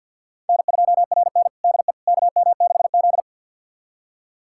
Morse code, heard as a keyed tone, about 51 wpm, 700 Hz. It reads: N2RN DE KM6Z
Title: USA Callsign DE Callsign - 4 Characters - 50wpm